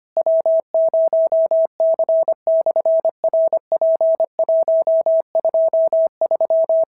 Morse code, heard as a keyed tone, about 25 wpm, 650 Hz. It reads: W0C/RP123
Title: SOTA QSO Practice 25wpm